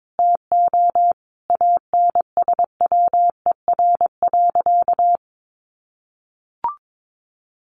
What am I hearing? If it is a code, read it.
Morse code, heard as a keyed tone, about 22 wpm, 700 Hz. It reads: TO ANSWER.
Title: Sentences from Top 1500 Words 22wpm